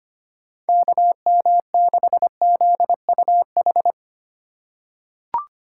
Morse code, heard as a keyed tone, about 25 wpm, 700 Hz. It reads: KM6ZUH
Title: USA Call Signs - 6 Characters - 4x Character Spacing - 25wpm